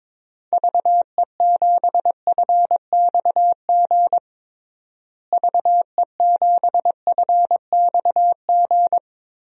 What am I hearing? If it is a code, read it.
Morse code, heard as a keyed tone, about 21 wpm, 700 Hz. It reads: VE7FXG VE7FXG
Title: DXCC-100 Call Signs - 6 Characters - Repeated 2x - 22wpm